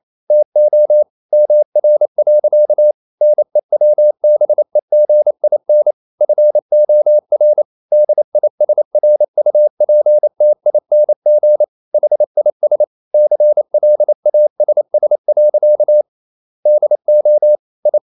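Morse code, heard as a keyed tone, about 28 wpm, 600 Hz. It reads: TO MR. NEWBEGIN FOR DISRUPTING HIS CLASS. DO I
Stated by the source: Morse code practice at 28 wpm, text of Ken Follett's novel Never in English, from CW med Gnister.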